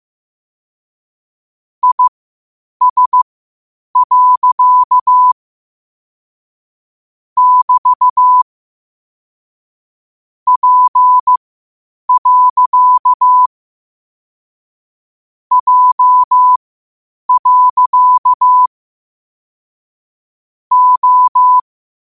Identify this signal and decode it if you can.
Morse code, heard as a keyed tone, about 15 wpm, spaced as for 8 wpm, 1000 Hz. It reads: IS. <BT> P. J. O
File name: qotdfiles / 8wpm